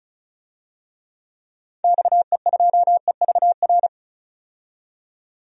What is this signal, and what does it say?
Morse code, heard as a keyed tone, about 34 wpm, 700 Hz. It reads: XE2EVR
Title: DXCC-100 Call Signs - 6 Characters - 35wpm